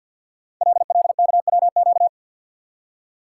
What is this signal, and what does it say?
Morse code, heard as a keyed tone, about 51 wpm, 700 Hz. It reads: LLKWQ
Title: Code Groups - 5 Letter:Numbers - 50wpm